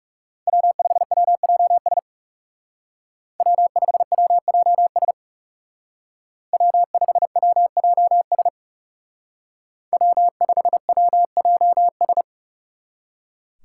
Morse code, heard as a keyed tone, about 36 wpm, 700 Hz. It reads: W5WJS W5WJS W5WJS W5WJS
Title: Speed Racing - USA 1x3 Call Signs 30wpm